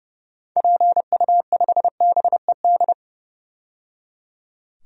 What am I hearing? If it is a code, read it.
Morse code, heard as a keyed tone, about 29 wpm, 700 Hz. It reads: PU5BED
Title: DXCC-100 Call Signs - 6 Characters - 30wpm